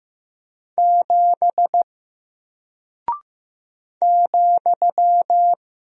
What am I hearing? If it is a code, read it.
Morse code, heard as a keyed tone, about 15 wpm, 700 Hz. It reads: 7 ,
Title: Single Character (Letter:Number:Punctuation:Prosign) Alt 15wpm